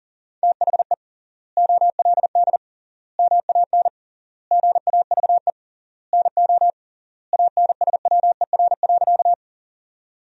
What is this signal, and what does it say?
Morse code, heard as a keyed tone, about 40 wpm, 700 Hz. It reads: THE OLD MAN GAVE NO ANSWER.
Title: 3x Word-Spacing - Sentences from Top 500 Words - 40wpm